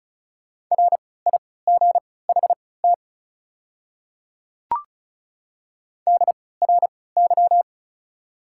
Morse code, heard as a keyed tone, about 36 wpm, spaced as for 20 wpm, 700 Hz. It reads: RIGHT DRY
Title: 3x Character Spacing - Top 500 Words - 35wpm